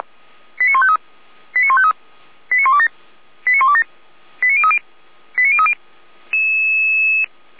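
Five-tone selective call (ZVEI), heard as quick runs of five tones, tones of about 70 ms. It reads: 89153 89153 89127 89127 89030 89030
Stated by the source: Five-tone ZVEI selective call received on SDR.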